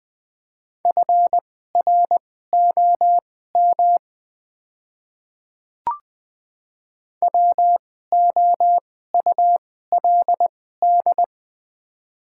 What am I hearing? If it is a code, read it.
Morse code, heard as a keyed tone, about 20 wpm, spaced as for 14 wpm, 700 Hz. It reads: FROM WOULD
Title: 2x Character Spacing - Top 100 Words - 20wpm